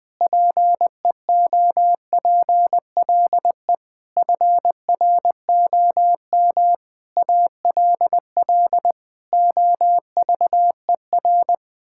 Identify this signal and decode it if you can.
Morse code, heard as a keyed tone, about 20 wpm, 700 Hz. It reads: PEOPLE FROM ALL OVER
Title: Difficult Sentences with Hints - 20wpm